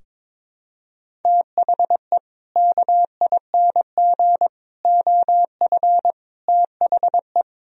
Morse code, heard as a keyed tone, about 22 wpm, 700 Hz. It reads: THE KING OF THE
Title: Sentences from Top 1000 Words - Review Part 2 - 22wpm